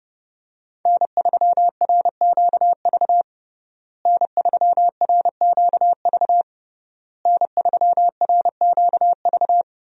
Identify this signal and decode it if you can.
Morse code, heard as a keyed tone, about 30 wpm, 700 Hz. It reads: N3RQV N3RQV N3RQV
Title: USA Call Signs - 5 Characters - Repeated 3x - 30wpm